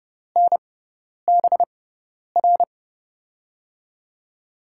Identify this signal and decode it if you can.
Morse code, heard as a keyed tone, about 30 wpm, spaced as for 10 wpm, 700 Hz. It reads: NBR
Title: USA Call Signs - 6x Character Spacing - 30wpm